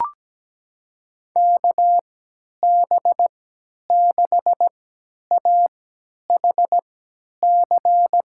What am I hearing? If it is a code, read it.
Morse code, heard as a keyed tone, about 17 wpm, spaced as for 10 wpm, 700 Hz. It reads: KB6AHC
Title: USA Call Signs - 6 Characters - 3x Character Spacing - 17wpm